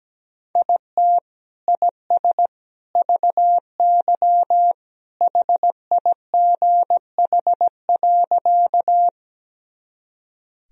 Morse code, heard as a keyed tone, about 17 wpm, 700 Hz. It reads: IT IS VY HIGH.
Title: Sentences from Top 200 Words and Abbreviations 17wpm